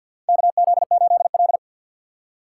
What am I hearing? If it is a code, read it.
Morse code, heard as a keyed tone, about 52 wpm, 700 Hz. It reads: KZ8L
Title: USA Call Signs - 4 Characters - 3x Character Spacing - 50wpm